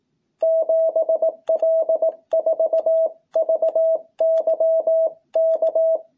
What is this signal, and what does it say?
Morse code, heard as a keyed tone, about 18 wpm, 660 Hz. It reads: 7L4VYK
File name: jh8cbh / 2021cw